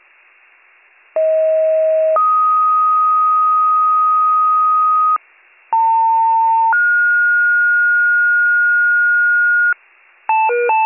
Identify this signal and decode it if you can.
Two-tone page, heard as a one-second tone, then a three-second tone, about 640 then 1250 Hz, about 900 then 1430 Hz, with noise. It is heard from 1.2 to 9.7 s.